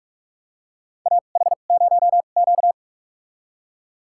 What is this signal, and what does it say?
Morse code, heard as a keyed tone, about 44 wpm, spaced as for 32 wpm, 690 Hz. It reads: AH0Q
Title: USA Call Signs - 4 Characters - 2x Character Spacing - 45wpm